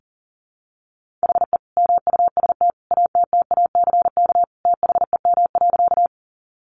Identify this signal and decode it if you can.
Morse code, heard as a keyed tone, about 40 wpm, 700 Hz. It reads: HE MUST ATTACK THEM.
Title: Sentences from Top 1500 Words 40wpm